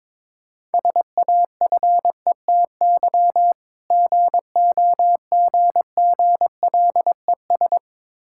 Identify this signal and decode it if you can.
Morse code, heard as a keyed tone, about 21 wpm, 700 Hz. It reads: SAFETY GOGGLES
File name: Difficult Words with Hints - 22wpm